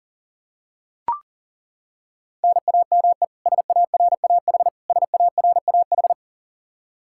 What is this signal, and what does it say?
Morse code, heard as a keyed tone, about 40 wpm, 700 Hz. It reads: NAME SARAH SARAH
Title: Practice QSO - Exchange 1 - Beginning - 40wpm